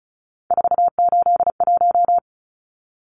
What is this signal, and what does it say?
Morse code, heard as a keyed tone, about 34 wpm, 700 Hz. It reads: 481
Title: Numbers - 3-Digit 35wpm